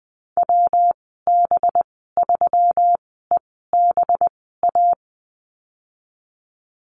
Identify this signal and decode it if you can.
Morse code, heard as a keyed tone, about 20 wpm, spaced as for 14 wpm, 700 Hz. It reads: WB3EBA
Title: 193 Lesson 31g - Letter v - Call Signs 20wpm